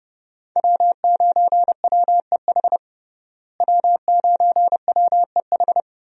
Morse code, heard as a keyed tone, about 30 wpm, 700 Hz. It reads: W9WEH W9WEH
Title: USA Call Signs - 5 Characters - Repeated 2x - 30wpm